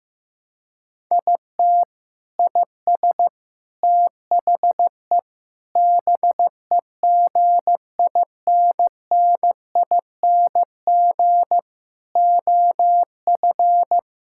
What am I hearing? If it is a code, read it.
Morse code, heard as a keyed tone, about 15 wpm, 700 Hz. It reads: IT IS THE BEGINNING OF
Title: Sentences from Top 1000 Words - Review Part 5 - 15wpm